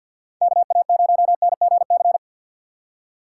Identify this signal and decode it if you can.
Morse code, heard as a keyed tone, about 62 wpm, 700 Hz. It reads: KA0NGC
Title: Speed Racing - USA Call Signs 50wpm